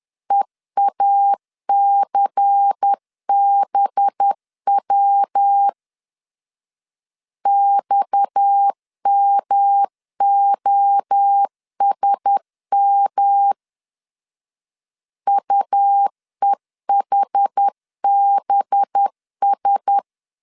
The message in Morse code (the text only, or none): EACBW XMOSM UEHBS